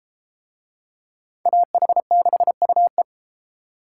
Morse code, heard as a keyed tone, about 31 wpm, 700 Hz. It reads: AH6UE